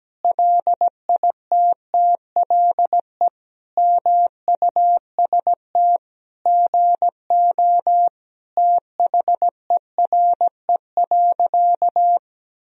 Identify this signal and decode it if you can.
Morse code, heard as a keyed tone, about 17 wpm, 700 Hz. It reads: LITTLE MUST GO THERE.
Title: Sentences from Top 1000 Words - Review Part 1 - 17wpm